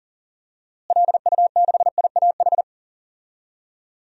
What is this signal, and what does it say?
Morse code, heard as a keyed tone, about 40 wpm, 700 Hz. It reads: LU6IAH
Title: DXCC-100 Call Signs - 6 Characters - Repeated 2x - 40wpm